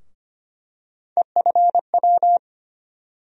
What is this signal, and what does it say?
Morse code, heard as a keyed tone, about 25 wpm, 700 Hz. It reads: EFW
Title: USA Call Sign Postfixes - 3 Letters 25wpm